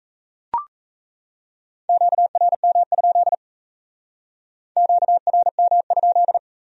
Morse code, heard as a keyed tone, about 36 wpm, 700 Hz. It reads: QRM? QRM?